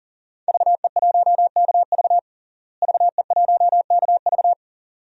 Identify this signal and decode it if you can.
Morse code, heard as a keyed tone, about 40 wpm, 700 Hz. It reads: VE1KV VE1KV
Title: DXCC-100 Call Signs - 5 Characters - Repeated 2x - 40wpm